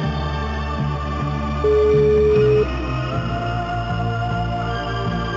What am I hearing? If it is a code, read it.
Ringback tone (Europe, 425 Hz), with noise.